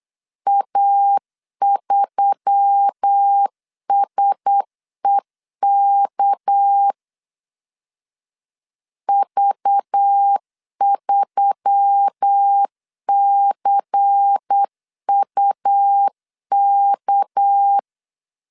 Morse code, heard as a keyed tone, about 8 wpm, 800 Hz. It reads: A3SEK V3CUK
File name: LEZIONI / 6 PKX0 / 6 PK 8wpm 3003